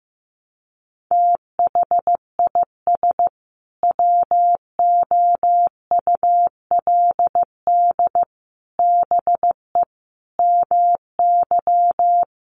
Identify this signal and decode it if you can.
Morse code, heard as a keyed tone, about 15 wpm, 700 Hz. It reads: THIS WOULD BE MY
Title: Sentences from Top 1200 Words 15wpm